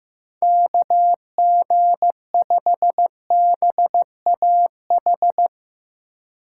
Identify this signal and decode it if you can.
Morse code, heard as a keyed tone, about 16 wpm, 700 Hz. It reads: KG5BAH
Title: Speed Racing - USA 2x3 Call Signs 15wpm